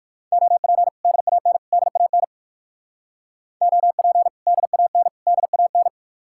Morse code, heard as a keyed tone, about 62 wpm, 700 Hz. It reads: OP DAN DAN OP DAN DAN